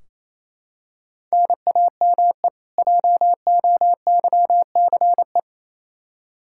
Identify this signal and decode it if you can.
Morse code, heard as a keyed tone, about 28 wpm, 700 Hz. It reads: NAME JOYCE
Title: K1USN SST QSO Practice 28wpm